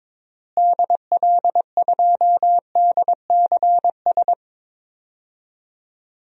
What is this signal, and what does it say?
Morse code, heard as a keyed tone, about 22 wpm, 700 Hz. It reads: DL2DCS
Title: DXCC-100 Call Signs - 6 Characters - Repeated 3x - 22wpm